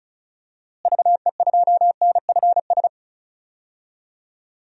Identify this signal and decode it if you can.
Morse code, heard as a keyed tone, about 34 wpm, 700 Hz. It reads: VE2NFS
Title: DXCC-100 Call Signs - 6 Characters - Repeated 2x - 35wpm